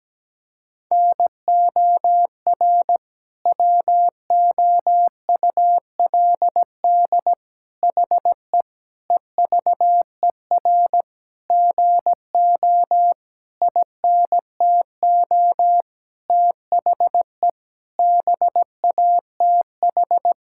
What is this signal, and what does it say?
Morse code, heard as a keyed tone, about 17 wpm, 700 Hz. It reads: NOR WOULD HE EVER GO INTO THE BATH
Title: Sentences from Top 1600 Words 17wpm